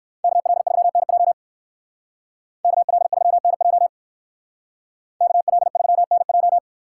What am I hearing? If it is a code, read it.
Morse code, heard as a keyed tone, about 58 wpm, 700 Hz. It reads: KL3NJ KL3NJ KL3NJ